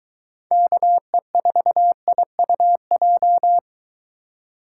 Morse code, heard as a keyed tone, about 23 wpm, 700 Hz. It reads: KE4IUJ